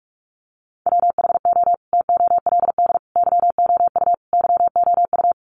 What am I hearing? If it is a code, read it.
Morse code, heard as a keyed tone, about 46 wpm, 700 Hz. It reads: WHO TOLD YOU YOU
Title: Sentences from Top 400 Words 45wpm